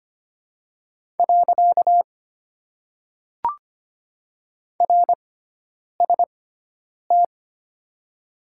Morse code, heard as a keyed tone, about 25 wpm, 700 Hz. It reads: . R S T
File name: Top 90 QSO Elements - 6x Character Spacing - 25wpm